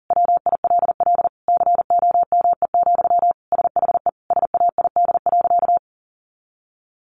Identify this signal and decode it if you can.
Morse code, heard as a keyed tone, about 40 wpm, 700 Hz. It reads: WILL COME, SHE SAID.
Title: Sentences from Top 100 Words 40wpm